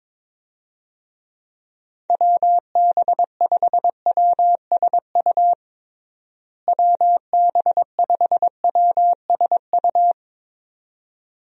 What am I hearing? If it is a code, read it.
Morse code, heard as a keyed tone, about 22 wpm, 700 Hz. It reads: WB5WSU WB5WSU
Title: USA Call Signs - 6 Characters - Repeated 2x - 22wpm